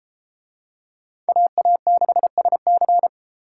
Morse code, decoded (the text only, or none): AA6SC